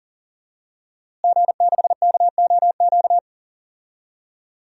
Morse code, heard as a keyed tone, about 40 wpm, 700 Hz. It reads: G6KOQ
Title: DXCC-100 Call Signs - 5 Characters - Repeated 2x - 40wpm